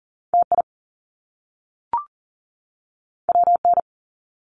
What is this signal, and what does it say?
Morse code, heard as a keyed tone, about 40 wpm, 700 Hz. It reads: TI WN
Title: Top Two Letter Combinations in Words 40wpm